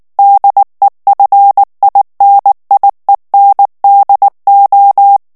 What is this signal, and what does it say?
Morse code, heard as a keyed tone, about 19 wpm, 800 Hz. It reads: DEFINIENDO
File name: morse mp3 / ESPAGNOL TEXTE EN CLAIR 2 VITESSE 18 MOTS MINUTES